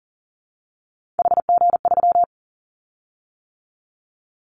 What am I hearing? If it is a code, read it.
Morse code, heard as a keyed tone, about 40 wpm, 700 Hz. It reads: HG3
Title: Callsign Prefixes for Top 50 DXCC - 40wpm